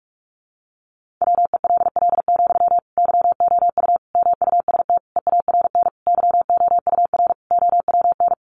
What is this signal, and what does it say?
Morse code, heard as a keyed tone, about 44 wpm, 700 Hz. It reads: WELL, YOU MUST EARN YOUR OWN
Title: Sentences from Top 1500 Words 45wpm